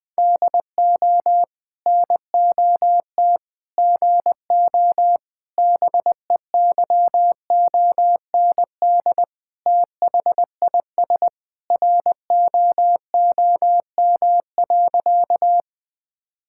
Morse code, heard as a keyed tone, about 21 wpm, 700 Hz. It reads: DO NOT GO BEYOND THIS ROOM.
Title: Sentences from Top 1000 Words - Review Part 6 - 20wpm